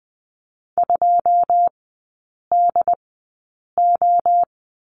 Morse code, heard as a keyed tone, about 20 wpm, spaced as for 8 wpm, 700 Hz. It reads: 2DO
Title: 121 Lesson 20e - Number 3 - Three Characters 20wpm